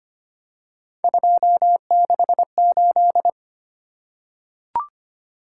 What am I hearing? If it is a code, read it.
Morse code, heard as a keyed tone, about 25 wpm, 700 Hz. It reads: 268
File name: Three Letter-Number Combinations - 25wpm